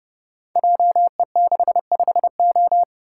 Morse code, heard as a keyed tone, about 30 wpm, 700 Hz. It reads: JE65O
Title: Code Groups - 5 Letter:Numbers - 30wpm